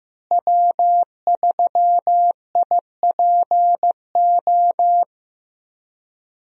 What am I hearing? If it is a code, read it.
Morse code, heard as a keyed tone, about 15 wpm, 700 Hz. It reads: W3IPO
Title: USA Call Signs - 5 Characters - Repeated 3x - 15wpm